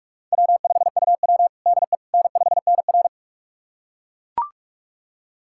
Morse code, heard as a keyed tone, about 41 wpm, 700 Hz. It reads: W5UW DE N5NR
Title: USA Callsign DE Callsign - 4 Characters - 45wpm